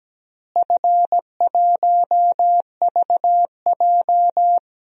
Morse code, heard as a keyed tone, about 17 wpm, 700 Hz. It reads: F1VJ
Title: DXCC-100 Call Signs - 4 Characters - Repeated 3x - 17wpm